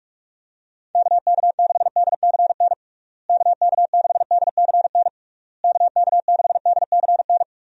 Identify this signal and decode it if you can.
Morse code, heard as a keyed tone, about 46 wpm, 700 Hz. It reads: KK6DCN KK6DCN KK6DCN
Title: USA Call Signs - 6 Characters - Repeated 3x - 45wpm